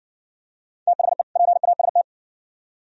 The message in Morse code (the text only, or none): THE PAST